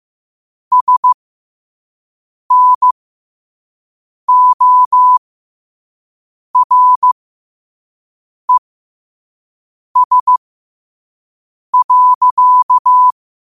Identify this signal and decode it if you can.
Morse code, heard as a keyed tone, about 15 wpm, spaced as for 5 wpm, 1000 Hz. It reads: SNORES.
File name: qotdfiles / 5wpm